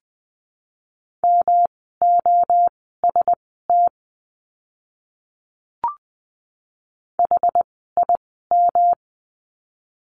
Morse code, heard as a keyed tone, about 20 wpm, spaced as for 14 wpm, 700 Hz. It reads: MOST HIM
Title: 230 Lesson 37f - Letter x - Words 20wpm